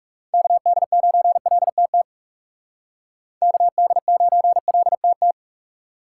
Morse code, heard as a keyed tone, about 37 wpm, 700 Hz. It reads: KD9LTT KD9LTT
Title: Speed Racing - USA 2x3 Call Signs 30wpm